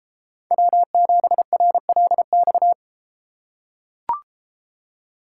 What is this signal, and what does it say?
Morse code, heard as a keyed tone, about 33 wpm, 700 Hz. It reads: W7RLX